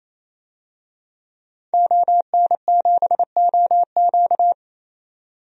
Morse code, heard as a keyed tone, about 28 wpm, 700 Hz. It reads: ON7OQ
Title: DXCC-100 Call Signs - 5 Characters - 28wpm